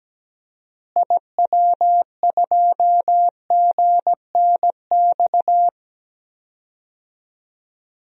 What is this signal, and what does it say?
Morse code, heard as a keyed tone, about 17 wpm, 700 Hz. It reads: IW2GNX
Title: DXCC-100 Call Signs - 6 Characters - Repeated 3x - 17wpm